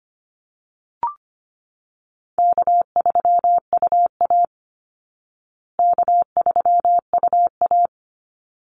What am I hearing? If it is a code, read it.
Morse code, heard as a keyed tone, about 25 wpm, 700 Hz. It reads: K3UA K3UA
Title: Call Signs Repeated 3x - 25wpm